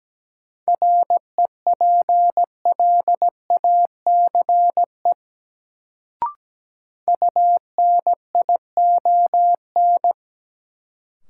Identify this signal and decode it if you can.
Morse code, heard as a keyed tone, about 17 wpm, 700 Hz. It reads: REPLACE UNION
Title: Sentences from Top 1400 Words - New Word Drill Alt - 17wpm